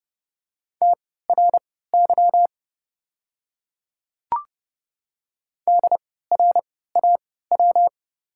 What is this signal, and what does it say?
Morse code, heard as a keyed tone, about 31 wpm, spaced as for 17 wpm, 700 Hz. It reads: TRY DRAW
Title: 3x Character Spacing - Top 500 Words - 30wpm